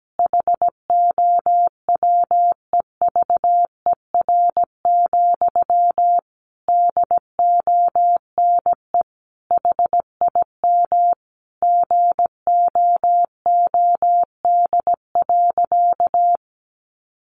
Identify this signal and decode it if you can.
Morse code, heard as a keyed tone, about 17 wpm, 700 Hz. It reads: HOWEVER, DONE HIM GOOD.
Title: Sentences from Top 1200 Words 17wpm